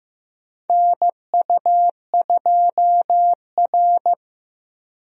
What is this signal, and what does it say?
Morse code, heard as a keyed tone, about 15 wpm, 700 Hz. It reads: NU2R
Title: USA Call Signs - 4 Characters - Repeated 3x - 15wpm